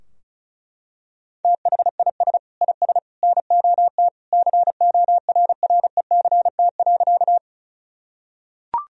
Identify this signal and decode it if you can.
Morse code, heard as a keyed tone, about 34 wpm, 700 Hz. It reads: THIS IS NOT CORRECT.
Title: Sentences from Top 1000 Words - Review Part 2 - 35wpm